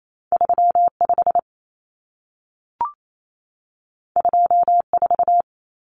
Morse code, heard as a keyed tone, about 28 wpm, 700 Hz. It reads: 35 24
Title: Common Cut Numbers - 2-Digit 28wpm